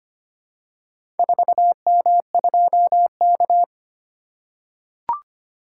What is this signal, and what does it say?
Morse code, heard as a keyed tone, about 25 wpm, 700 Hz. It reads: 4M2K
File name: practice / 25wpm / Four Letter-Number Combinations - 25wpm